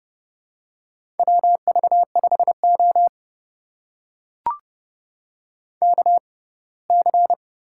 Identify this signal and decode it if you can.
Morse code, heard as a keyed tone, about 30 wpm, spaced as for 20 wpm, 700 Hz. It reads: WV5O KC
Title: USA Call Signs - 6x Character Spacing - 30wpm